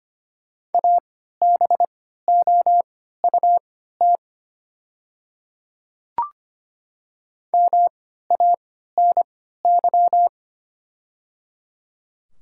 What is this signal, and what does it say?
Morse code, heard as a keyed tone, about 25 wpm, spaced as for 14 wpm, 700 Hz. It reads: ABOUT MANY
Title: 3x Character Spacing - Top 100 Words - 25wpm